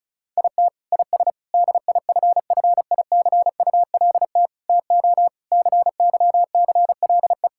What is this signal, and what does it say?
Morse code, heard as a keyed tone, about 36 wpm, 700 Hz. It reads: IT IS DIFFICULT TO CYCLE
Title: Sentences from Top 1000 Words - Review Part 6 - 35wpm